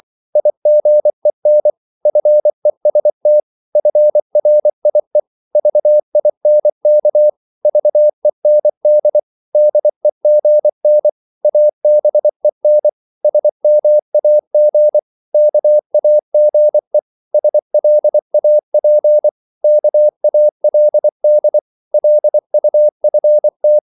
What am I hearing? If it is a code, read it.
Morse code, heard as a keyed tone, about 24 wpm, 600 Hz. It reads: IGEN FEST FRIE VINK VEND DEGN ABEN SMAG KAGE SLAP KALD LUFT
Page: Korte danske ord 24wpm